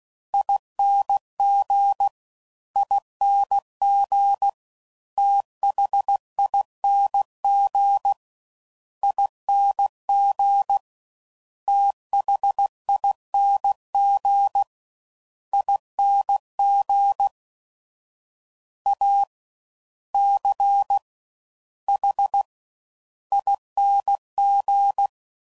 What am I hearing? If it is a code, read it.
Morse code, heard as a keyed tone, about 16 wpm, 800 Hz. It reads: ING ING THING ING THING ING A C H ING
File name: Suffixes / Special INGs / Fast Ings